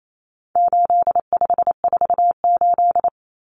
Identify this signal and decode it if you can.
Morse code, heard as a keyed tone, about 28 wpm, 700 Hz. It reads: 8548
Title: Numbers - 4-Digit 28wpm